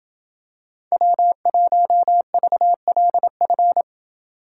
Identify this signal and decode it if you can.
Morse code, heard as a keyed tone, about 27 wpm, 700 Hz. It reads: W1VLF